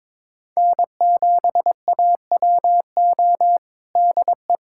Morse code, heard as a keyed tone, about 22 wpm, 700 Hz. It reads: N7AWO DE
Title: USA Callsign DE Callsign 22wpm